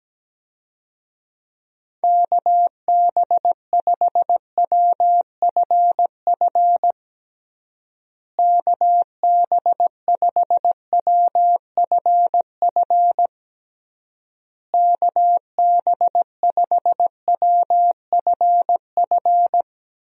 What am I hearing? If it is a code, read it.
Morse code, heard as a keyed tone, about 17 wpm, 700 Hz. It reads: KB5WFF KB5WFF KB5WFF
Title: USA Call Signs - 6 Characters - Repeated 3x - 17wpm